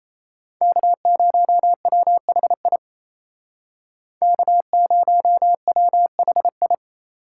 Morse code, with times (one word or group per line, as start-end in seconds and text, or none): 0.6-2.8 K0WHI
4.2-6.7 K0WHI